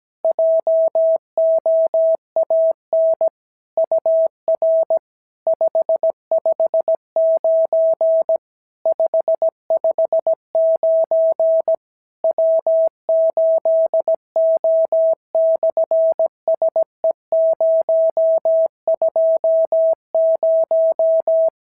Morse code, heard as a keyed tone, about 17 wpm, 650 Hz. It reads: JOAN UR 559 559 W8O/SE020
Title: SOTA QSO Practice 17wpm